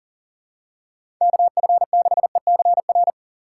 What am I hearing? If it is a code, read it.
Morse code, heard as a keyed tone, about 40 wpm, 700 Hz. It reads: KF6ECR
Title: USA Call Signs - 4x Character Spacing - 40wpm